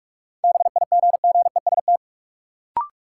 Morse code, heard as a keyed tone, about 42 wpm, 700 Hz. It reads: BIGGEST
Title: Sets of 4 Words - New Word Drill - 45wpm